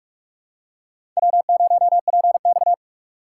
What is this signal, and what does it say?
Morse code, heard as a keyed tone, about 41 wpm, 700 Hz. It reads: W0PX